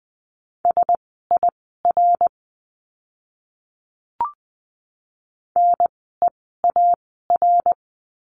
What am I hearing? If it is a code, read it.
Morse code, heard as a keyed tone, about 20 wpm, spaced as for 15 wpm, 700 Hz. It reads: SIR NEAR
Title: 054 Lesson 10f - Letter r - Words 20wpm